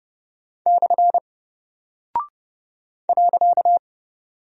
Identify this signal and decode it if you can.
Morse code, heard as a keyed tone, about 30 wpm, 700 Hz. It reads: / .
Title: Single Character (Letter:Number:Punctuation:Prosign) Alt 30wpm